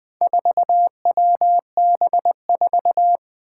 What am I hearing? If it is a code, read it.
Morse code, heard as a keyed tone, about 20 wpm, 700 Hz. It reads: 4WB4
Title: Four Challenging Letter-Number Combinations 20wpm